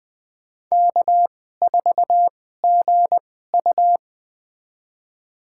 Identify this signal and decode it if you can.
Morse code, heard as a keyed tone, about 21 wpm, spaced as for 15 wpm, 700 Hz. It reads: K4GU
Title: USA Call Signs - 4 Characters - 2x Character Spacing - 20wpm